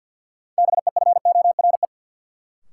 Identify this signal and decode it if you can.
Morse code, heard as a keyed tone, about 48 wpm, 700 Hz. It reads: BEFORE
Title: Sentences from Top 200 Words - New Word Drill - 50wpm